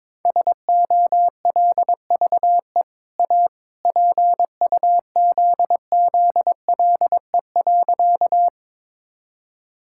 Morse code, heard as a keyed tone, about 22 wpm, 700 Hz. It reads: SOLVE A PUZZLE.
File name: practice / 22wpm / Easy Sentences with Hints - 22wpm